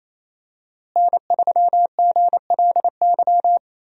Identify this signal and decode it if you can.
Morse code, heard as a keyed tone, about 28 wpm, 700 Hz. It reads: N3GLY
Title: USA Call Signs - 5 Characters - Repeated 3x - 28wpm